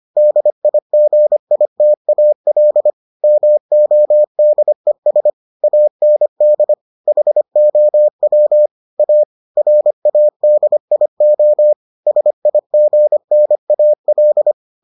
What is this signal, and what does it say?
Morse code, heard as a keyed tone, about 25 wpm, 600 Hz. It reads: DIGITAL MODES AND HOW A RADIO SIGNAL